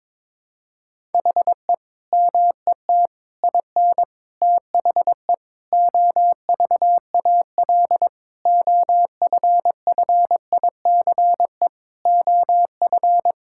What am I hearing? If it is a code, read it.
Morse code, heard as a keyed tone, about 22 wpm, 700 Hz. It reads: HE MET IN THE OVAL OFFICE OF
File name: practice / 22wpm / Sentences with 8-Word Phrases - 22wpm